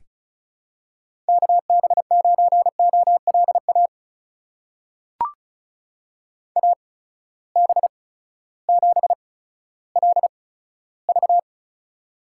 Morse code, heard as a keyed tone, about 36 wpm, spaced as for 16 wpm, 700 Hz. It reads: KB9OLA AB7LV
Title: USA Call Signs - 8x Character Spacing - 35wpm